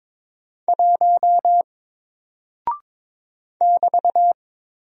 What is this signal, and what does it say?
Morse code, heard as a keyed tone, about 22 wpm, 700 Hz. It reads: 1 <BT>
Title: Single Character (Letter:Number:Punctuation:Prosign) Alt 22wpm